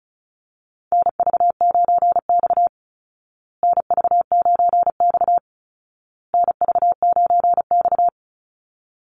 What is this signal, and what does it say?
Morse code, heard as a keyed tone, about 34 wpm, 700 Hz. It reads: NV9X NV9X NV9X
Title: Call Signs Repeated 2x - 35wpm